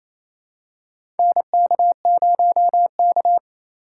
Morse code, heard as a keyed tone, about 28 wpm, 700 Hz. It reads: NK0K
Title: USA Call Signs - 2x Character Spacing - 28wpm